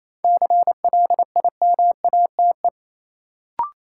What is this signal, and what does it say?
Morse code, heard as a keyed tone, about 27 wpm, 700 Hz. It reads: CLIMATE
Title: Sentences from Top 1600 Words - New Word Drill Alt - 28wpm